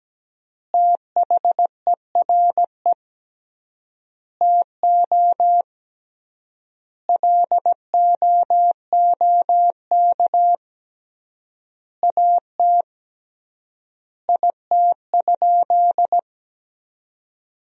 Morse code, heard as a keyed tone, about 17 wpm, 700 Hz. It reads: THERE TO LOOK AT IT?
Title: 3x Word-Spacing - Sentences from Top 100 Words - 17wpm